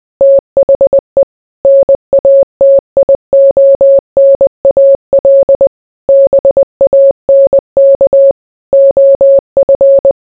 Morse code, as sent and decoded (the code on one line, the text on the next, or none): - .... / -. .- - .. --- -. .- .-.. / -... .- -. -.- / --- ..-.
THE NATIONAL BANK OF